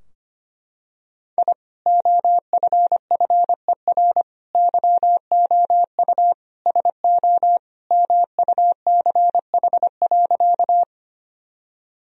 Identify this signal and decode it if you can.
Morse code, heard as a keyed tone, about 25 wpm, 700 Hz. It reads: I OFFER YOU SO MUCH.
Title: Sentences from Top 1000 Words - Review Part 4 - 25wpm